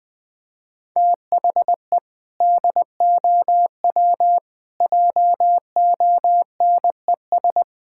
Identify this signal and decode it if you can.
Morse code, heard as a keyed tone, about 20 wpm, 700 Hz. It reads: THE DOW JONES
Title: Build A Phrase - 5-Word Phrases - 20wpm